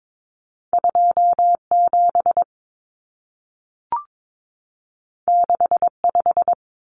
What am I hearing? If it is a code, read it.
Morse code, heard as a keyed tone, about 22 wpm, 700 Hz. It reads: 27 65
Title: Numbers - 2-Digit 22wpm